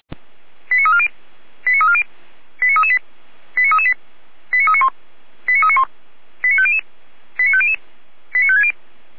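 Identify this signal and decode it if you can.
Five-tone selective call (ZVEI), heard as quick runs of five tones, tones of about 70 ms. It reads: 89340 89340 89338 89338 89391 89391 89500 89500 89560
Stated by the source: ZVEI five-tone call received by SDR, from the feuerwehr-orlamuende.org fire brigade audio archive.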